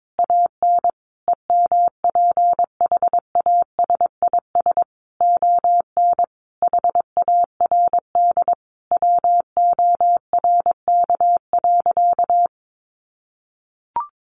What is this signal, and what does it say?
Morse code, heard as a keyed tone, about 22 wpm, 700 Hz. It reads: AN EMPHASIS ON HARD WORK.
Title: Sentences from Top 1400 Words 22wpm